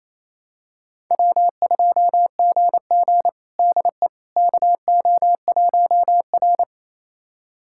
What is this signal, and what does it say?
Morse code, heard as a keyed tone, about 28 wpm, 700 Hz. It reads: W2GG DE KO1R
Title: USA Callsign DE Callsign - 4 Characters - 28wpm